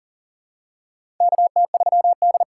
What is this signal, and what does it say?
Morse code, heard as a keyed tone, about 40 wpm, 700 Hz. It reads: KT3D
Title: USA Call Signs - 4 Characters - 6x Character Spacing - 40wpm